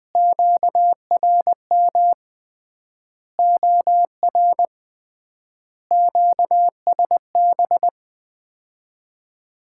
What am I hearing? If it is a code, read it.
Morse code, heard as a keyed tone, about 20 wpm, 700 Hz. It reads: QRM OR QSB
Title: Common QSO Phrases - 3x Word-Spacing - 20wpm